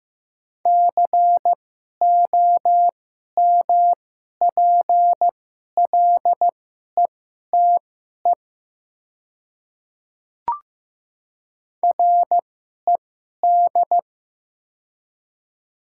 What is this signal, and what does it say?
Morse code, heard as a keyed tone, about 15 wpm, spaced as for 11 wpm, 700 Hz. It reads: COMPLETE RED
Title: 2x Character Spacing - Top 500 Words - 15wpm